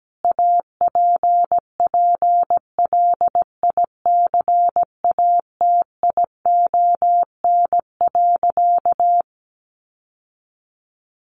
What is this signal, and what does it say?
Morse code, heard as a keyed tone, about 17 wpm, 700 Hz. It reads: APPLICATION.
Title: Sentences from Top 1200 Words 17wpm